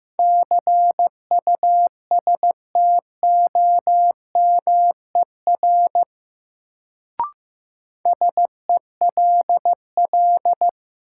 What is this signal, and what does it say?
Morse code, heard as a keyed tone, about 15 wpm, 700 Hz. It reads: CUSTOMER SELL
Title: Sentences from Top 1100 Words - New Word Drill Alt - 15wpm